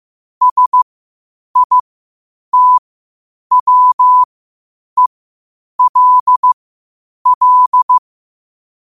Morse code, heard as a keyed tone, about 15 wpm, spaced as for 8 wpm, 1000 Hz. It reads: SITWELL
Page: 8 wpm morse code quotes